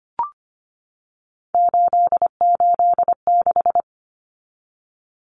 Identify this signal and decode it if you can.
Morse code, heard as a keyed tone, about 25 wpm, 700 Hz. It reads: 886
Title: Numbers - 3-Digit 25wpm